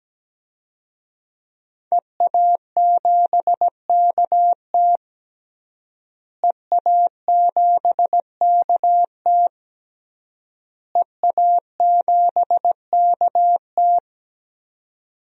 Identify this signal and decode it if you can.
Morse code, heard as a keyed tone, about 17 wpm, 700 Hz. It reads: EA7KT EA7KT EA7KT
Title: DXCC-100 Call Signs - 5 Characters - Repeated 3x - 17wpm